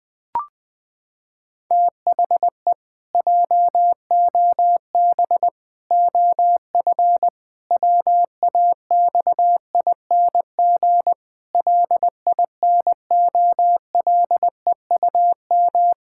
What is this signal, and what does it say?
Morse code, heard as a keyed tone, about 21 wpm, 700 Hz. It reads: THE JOB OF WAXING LINOLEUM
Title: Pangram Challenge v2 - 20wpm